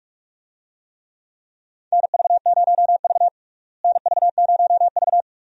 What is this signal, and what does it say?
Morse code, heard as a keyed tone, about 41 wpm, 700 Hz. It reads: NV0V NV0V
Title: USA Call Signs - 4 Characters - Repeated 2x - 45wpm